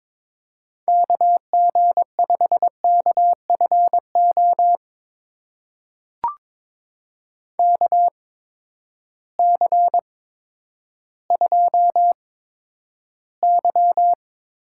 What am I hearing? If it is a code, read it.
Morse code, heard as a keyed tone, about 23 wpm, spaced as for 11 wpm, 700 Hz. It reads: KG5KFO KC2Y
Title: USA Call Signs - 6 Characters - 8x Character Spacing - 22wpm